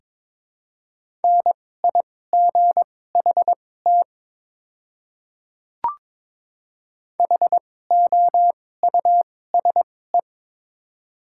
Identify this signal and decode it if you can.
Morse code, heard as a keyed tone, about 22 wpm, spaced as for 16 wpm, 700 Hz. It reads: NIGHT HOUSE
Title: 2x Character Spacing - Top 300 Words - 22wpm